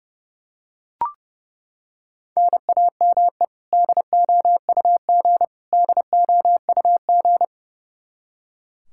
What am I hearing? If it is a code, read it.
Morse code, heard as a keyed tone, about 32 wpm, 700 Hz. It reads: NAME DOUG DOUG
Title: Practice QSO - Exchange 1 - Beginning - 30wpm